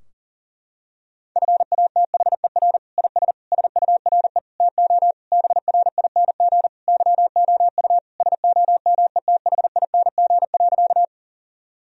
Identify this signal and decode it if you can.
Morse code, heard as a keyed tone, about 37 wpm, 700 Hz. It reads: FATHER IS SURE TO BRING YOU SOMETHING.
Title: Sentences from Top 1000 Words - Review Part 3 - 40wpm